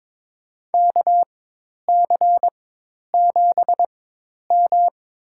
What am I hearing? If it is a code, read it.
Morse code, heard as a keyed tone, about 22 wpm, spaced as for 10 wpm, 700 Hz. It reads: KC7M
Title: USA Call Signs - 4x Character Spacing - 22wpm